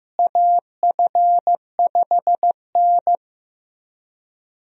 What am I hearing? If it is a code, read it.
Morse code, heard as a keyed tone, about 16 wpm, 700 Hz. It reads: AF5N
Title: USA Call Signs - 4 Characters - Repeated 2x - 15wpm